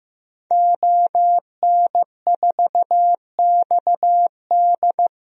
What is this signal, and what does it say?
Morse code, heard as a keyed tone, about 15 wpm, 700 Hz. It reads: ON4XD
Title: DXCC-100 Call Signs - 5 Characters - 15wpm